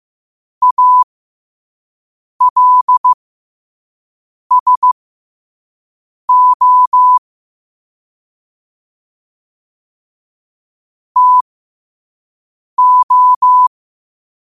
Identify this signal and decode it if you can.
Morse code, heard as a keyed tone, about 15 wpm, spaced as for 5 wpm, 1000 Hz. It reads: ALSO TO
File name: qotdfiles / 5wpm